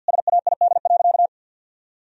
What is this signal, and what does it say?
Morse code, heard as a keyed tone, about 52 wpm, 700 Hz. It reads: SAID.
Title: Sentences from Top 1000 Words - Review Part 2 - 50wpm